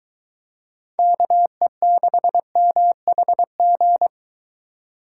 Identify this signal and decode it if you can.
Morse code, heard as a keyed tone, about 23 wpm, 700 Hz. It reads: KE6MHG